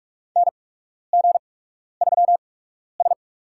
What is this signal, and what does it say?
Morse code, heard as a keyed tone, about 46 wpm, spaced as for 12 wpm, 700 Hz. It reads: NG3S
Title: USA Call Signs - 4 Characters - 8x Character Spacing - 45wpm